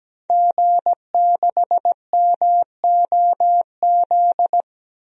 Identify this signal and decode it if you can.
Morse code, heard as a keyed tone, about 17 wpm, 700 Hz. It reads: G6MOZ
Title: DXCC-100 Call Signs - 5 Characters - 17wpm